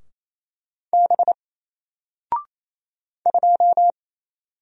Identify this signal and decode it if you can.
Morse code, heard as a keyed tone, about 27 wpm, 700 Hz. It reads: B 2
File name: Single-Letter-Number-28wpm